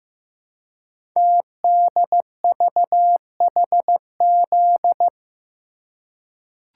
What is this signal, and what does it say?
Morse code, heard as a keyed tone, about 15 wpm, 700 Hz. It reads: TDVHZ
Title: Code Groups - 5 Letters - 15wpm